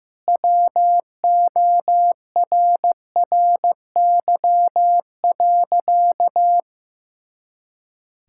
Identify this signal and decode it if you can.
Morse code, heard as a keyed tone, about 16 wpm, 700 Hz. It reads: WORRY.
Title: Pangram Challenge v2 - 15wpm